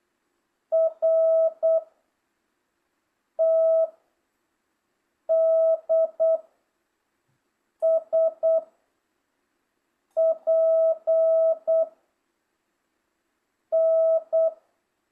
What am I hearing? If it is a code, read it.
Morse code, heard as a keyed tone, about 8 wpm, 660 Hz. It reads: R T D S P N